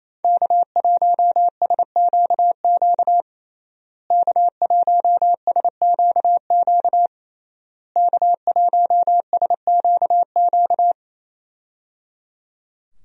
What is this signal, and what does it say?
Morse code, heard as a keyed tone, about 27 wpm, 700 Hz. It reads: K1SQQ K1SQQ K1SQQ
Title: USA Call Signs - 5 Characters - Repeated 3x - 28wpm